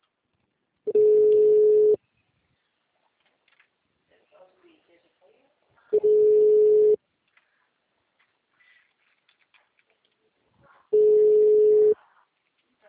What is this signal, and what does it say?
Ringback tone (Europe, 425 Hz), with speech.